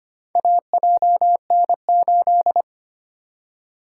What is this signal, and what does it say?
Morse code, heard as a keyed tone, about 25 wpm, 700 Hz. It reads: AJN8
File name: practice / 25wpm / Four Letter-Number Combinations - 25wpm